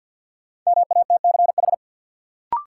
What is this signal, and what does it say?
Morse code, heard as a keyed tone, about 52 wpm, 700 Hz. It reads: MATCH
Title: Sentences from Top 1600 Words - New Word Drill Alt - 50wpm